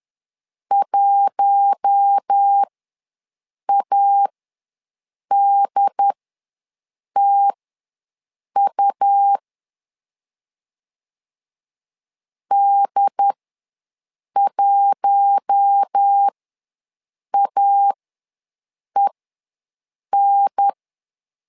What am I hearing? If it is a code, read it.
Morse code, heard as a keyed tone, about 10 wpm, spaced as for 6 wpm, 800 Hz. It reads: 1ADTU D1AEN
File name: LEZIONI / 4 Q1 ws 10wpm 2538